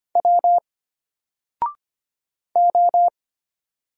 Morse code, heard as a keyed tone, about 25 wpm, 700 Hz. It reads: W O
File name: Single-Letter-Number-25wpm